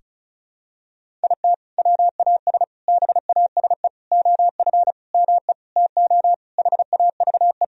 Morse code, heard as a keyed tone, about 34 wpm, 700 Hz. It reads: IT WAS BASE OF ME TO HAVE
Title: Sentences from Top 1000 Words - Review Part 1 - 35wpm